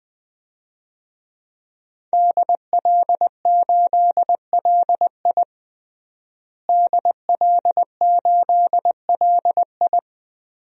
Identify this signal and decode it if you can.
Morse code, heard as a keyed tone, about 19 wpm, 700 Hz. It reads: DL8LI DL8LI
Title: DXCC-100 Call Signs - 5 Characters - Repeated 3x - 20wpm